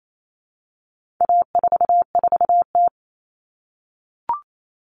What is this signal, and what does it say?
Morse code, heard as a keyed tone, about 28 wpm, 700 Hz. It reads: A44T
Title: All Cut Numbers - 4-Digit 28wpm